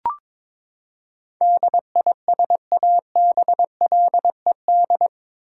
Morse code, heard as a keyed tone, about 22 wpm, 700 Hz. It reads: DISABLED
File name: practice / 22wpm / Eight Letter Words - 22wpm